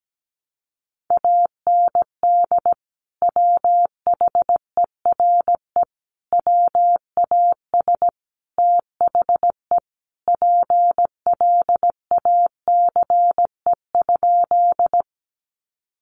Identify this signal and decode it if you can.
Morse code, heard as a keyed tone, about 17 wpm, 700 Hz. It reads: AND WHERE WAS THE PLACE?
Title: Sentences from Top 200 Words 17wpm